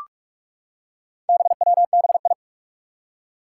Morse code, heard as a keyed tone, about 46 wpm, 700 Hz. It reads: BWBI
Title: Four Challenging Letter-Number Combinations 45wpm